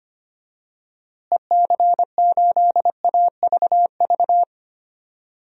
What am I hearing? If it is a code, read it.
Morse code, heard as a keyed tone, about 25 wpm, 700 Hz. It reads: EC8AVV
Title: DXCC-100 Call Signs - 6 Characters - Repeated 3x - 25wpm